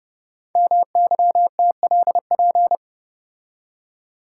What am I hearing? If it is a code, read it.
Morse code, heard as a keyed tone, about 30 wpm, 700 Hz. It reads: MYTLP
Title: Code Groups - 5 Letters - 30wpm